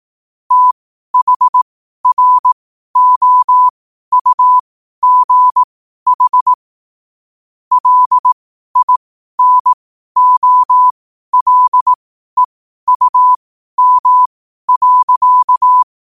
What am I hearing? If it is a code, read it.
Morse code, heard as a keyed tone, about 18 wpm, spaced as for 12 wpm, 1000 Hz. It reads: THROUGH LINOLEUM.
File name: qotdfiles / 12wpm